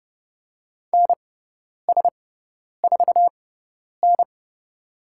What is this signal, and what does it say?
Morse code, heard as a keyed tone, about 30 wpm, spaced as for 10 wpm, 700 Hz. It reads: NS4N
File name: Lesson 16g - Letter c - Call Signs 30wpm